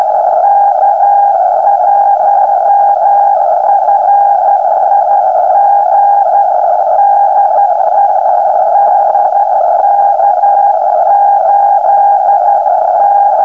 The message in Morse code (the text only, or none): none